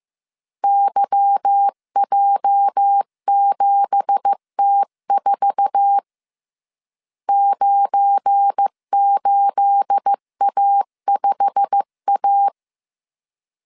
Morse code, heard as a keyed tone, about 15 wpm, 800 Hz. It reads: YJ7T4 98A5A
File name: CW / DET / 1 9 1500C S B 14wpm 3302